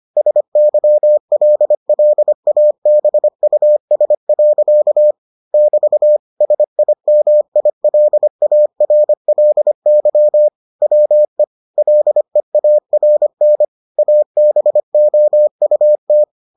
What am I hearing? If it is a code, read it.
Morse code, heard as a keyed tone, about 25 wpm, 600 Hz. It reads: SYLLABUS. <BT> SIMILARLY WE LEARN ABOUT